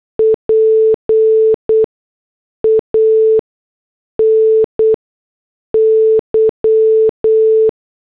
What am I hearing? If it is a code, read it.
Morse code, heard as a keyed tone, about 8 wpm, spaced as for 6 wpm, 440 Hz. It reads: PANY
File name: morse / headlines